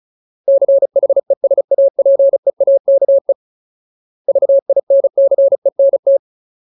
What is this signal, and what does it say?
Morse code, heard as a keyed tone, about 34 wpm, 550 Hz. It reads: CHESAPEAKE VINCENT
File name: Ultimate Word List / 35WPM Ultimate